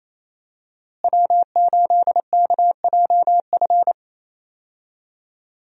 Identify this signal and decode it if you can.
Morse code, heard as a keyed tone, about 28 wpm, 700 Hz. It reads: W8KJF
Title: USA Call Signs - 5 Characters - Repeated 3x - 28wpm